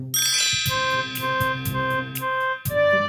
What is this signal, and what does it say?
There is no code.